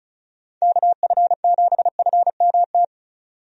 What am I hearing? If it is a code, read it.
Morse code, heard as a keyed tone, about 34 wpm, 700 Hz. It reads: KF7FMT
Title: USA Call Signs - 6 Characters - 35wpm